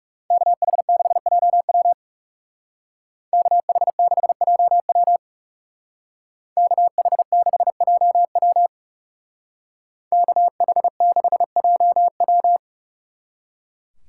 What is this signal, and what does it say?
Morse code, heard as a keyed tone, about 36 wpm, 700 Hz. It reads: KH6JW KH6JW KH6JW KH6JW
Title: Speed Racing - USA 2x2 Call Signs 30wpm